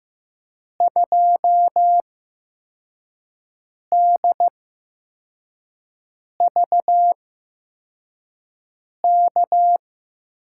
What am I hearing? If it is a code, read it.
Morse code, heard as a keyed tone, about 15 wpm, spaced as for 4 wpm, 700 Hz. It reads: 2DVK
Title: USA Call Signs - 6 Characters - 8x Character Spacing - 15wpm